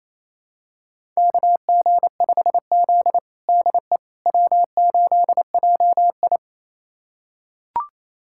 Morse code, heard as a keyed tone, about 28 wpm, 700 Hz. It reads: KG5Z DE W8JI
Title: USA Callsign DE Callsign - 4 Characters - 28wpm